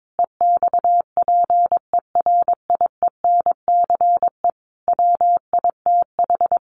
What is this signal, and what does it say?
Morse code, heard as a keyed tone, about 22 wpm, 700 Hz. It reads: EXPERIENCE WITH
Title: Sentences from Top 600 Words 22wpm